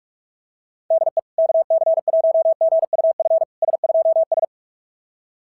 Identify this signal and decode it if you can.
Morse code, heard as a keyed tone, about 44 wpm, 660 Hz. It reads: DE KC1GUF S2S
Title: SOTA QSO Practice 45wpm